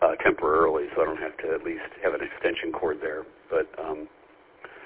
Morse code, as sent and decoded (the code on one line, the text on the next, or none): none